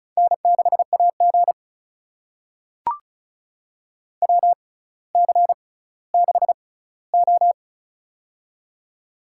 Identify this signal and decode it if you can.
Morse code, heard as a keyed tone, about 36 wpm, spaced as for 18 wpm, 700 Hz. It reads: N6AG WC6O
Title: USA Call Signs - 4 Characters - 6x Character Spacing - 35wpm